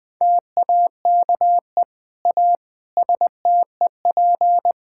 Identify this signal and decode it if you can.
Morse code, heard as a keyed tone, about 21 wpm, 700 Hz. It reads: TAKE A STEP
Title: Sentences with 8-Word Phrases - 20wpm